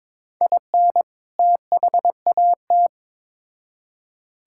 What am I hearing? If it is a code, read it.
Morse code, heard as a keyed tone, about 21 wpm, 700 Hz. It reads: IN THAT
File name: Build A Phrase - 2-Word Phrases - 22wpm